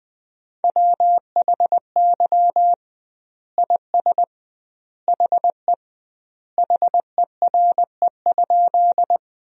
Morse code, heard as a keyed tone, about 20 wpm, 700 Hz. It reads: WHY IS HE HERE?
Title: 2x Word-Spacing - Sentences from Top 200 Words - 20wpm